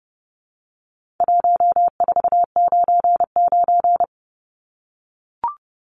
Morse code, heard as a keyed tone, about 30 wpm, 700 Hz. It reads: 1499
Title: Numbers - 4-Digit 30wpm